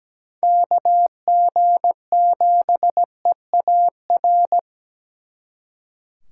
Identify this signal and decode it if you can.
Morse code, heard as a keyed tone, about 20 wpm, 700 Hz. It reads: KG7EAR
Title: Speed Racing - USA 2x3 Call Signs 17wpm